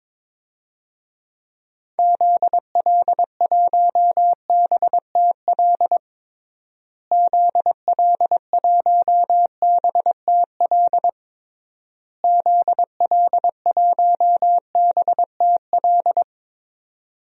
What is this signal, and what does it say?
Morse code, heard as a keyed tone, about 22 wpm, 700 Hz. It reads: ZL1BTL ZL1BTL ZL1BTL
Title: DXCC-100 Call Signs - 6 Characters - Repeated 3x - 22wpm